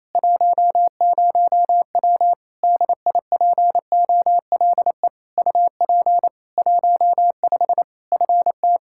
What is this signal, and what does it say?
Morse code, heard as a keyed tone, about 28 wpm, 700 Hz. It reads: 10W DIPOLE UP 15 FT
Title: Practice QSO - Exchange 2 - Info - 28wpm